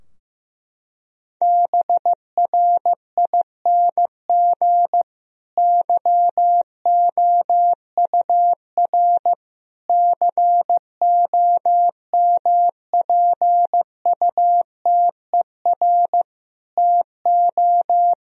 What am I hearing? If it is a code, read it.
Morse code, heard as a keyed tone, about 16 wpm, 700 Hz. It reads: BRING YOUR COMPUTER TO
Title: Sentences from Top 1000 Words - Review Part 3 - 15wpm